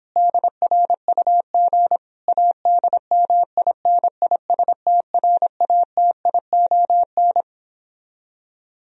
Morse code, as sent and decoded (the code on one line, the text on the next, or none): -.. .-. ..- --. / .- -.. -- .. -. .. ... - .-. .- - .. --- -.
DRUG ADMINISTRATION